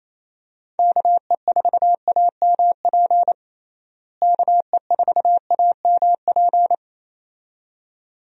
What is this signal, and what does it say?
Morse code, heard as a keyed tone, about 27 wpm, 700 Hz. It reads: KE4AMP KE4AMP
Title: USA Call Signs - 6 Characters - Repeated 2x - 28wpm